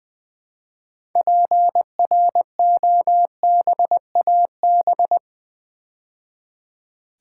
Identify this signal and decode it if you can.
Morse code, heard as a keyed tone, about 20 wpm, 700 Hz. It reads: PROBAB
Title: Build A Word - 3 Letters At A Time - 8-Letter Words - 20wpm